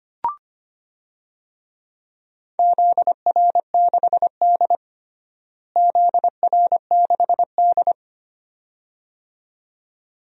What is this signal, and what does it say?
Morse code, heard as a keyed tone, about 25 wpm, 700 Hz. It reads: ZR6D ZR6D
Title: DXCC-100 Call Signs - 4 Characters - Repeated 2x - 25wpm